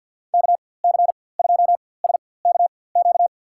Morse code, heard as a keyed tone, about 52 wpm, spaced as for 23 wpm, 700 Hz. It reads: KC2SKQ
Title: USA Call Signs - 4x Character Spacing - 50wpm